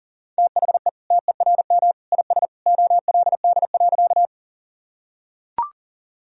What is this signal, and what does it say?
Morse code, heard as a keyed tone, about 40 wpm, 700 Hz. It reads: THE TERM IS OLD.
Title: Sentences from Top 1000 Words - Review Part 4 - 40wpm